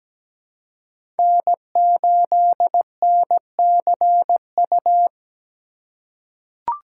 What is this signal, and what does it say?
Morse code, heard as a keyed tone, about 17 wpm, 700 Hz. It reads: N8NCU
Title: USA Call Signs - 5 Characters - 2x Character Spacing - 17wpm